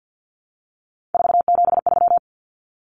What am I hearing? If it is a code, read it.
Morse code, heard as a keyed tone, about 50 wpm, 700 Hz. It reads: 473
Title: Numbers - 3-Digit 50wpm